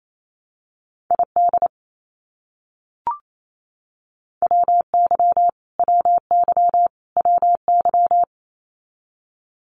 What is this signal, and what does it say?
Morse code, heard as a keyed tone, about 28 wpm, 700 Hz. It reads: ID WY WY WY
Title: US States Repeated 3x 28wpm